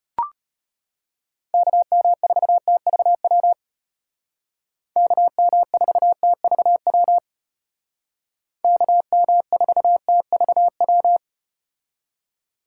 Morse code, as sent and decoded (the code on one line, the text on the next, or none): -.- -- ....- - ...- .-- / -.- -- ....- - ...- .-- / -.- -- ....- - ...- .--
KM4TVW KM4TVW KM4TVW